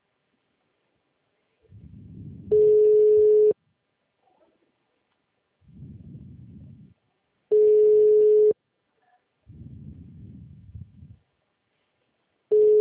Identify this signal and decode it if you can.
Ringback tone (Europe, 425 Hz).